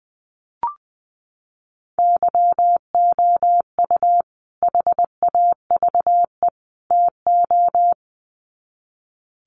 Morse code, heard as a keyed tone, about 20 wpm, 700 Hz. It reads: YOU HAVE TO
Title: Sets of 3 Words 20wpm